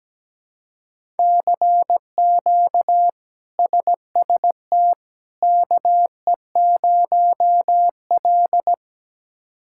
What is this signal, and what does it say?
Morse code, heard as a keyed tone, about 17 wpm, 700 Hz. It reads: CQ SST KE0L
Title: K1USN SST QSO Practice - 17wpm